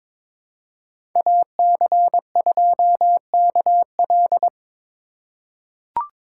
Morse code, heard as a keyed tone, about 22 wpm, 700 Hz. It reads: AC2KL
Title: USA Call Signs - 5 Characters - 4x Character Spacing - 22wpm